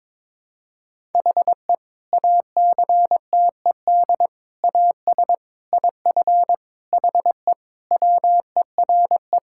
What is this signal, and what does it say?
Morse code, heard as a keyed tone, about 22 wpm, 700 Hz. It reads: HE ACTED AS IF HE WERE
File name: practice / 22wpm / Sentences with 4-Word Phrases - 22wpm